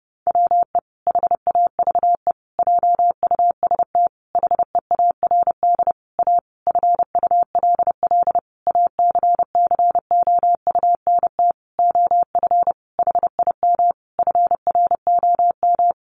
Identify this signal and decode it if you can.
Morse code, heard as a keyed tone, about 30 wpm, 700 Hz. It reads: WE HAVE JUST HEARD A FULL ACCOUNT OF HIM FROM
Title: Sentences from Top 900 Words 30wpm